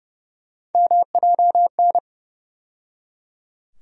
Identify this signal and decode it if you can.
Morse code, heard as a keyed tone, about 30 wpm, 700 Hz. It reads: MJN
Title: USA Call Sign Postfixes - 3 Letters 30wpm